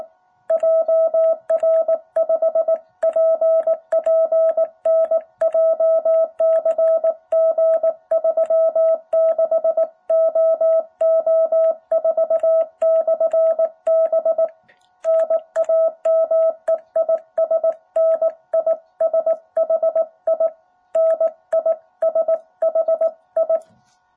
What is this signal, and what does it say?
Morse code, heard as a keyed tone, about 18 wpm, 660 Hz. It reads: JR5PPN JCG 36OO4/B NAME IS NISHI NISHI
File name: jh8cbh / 2021cw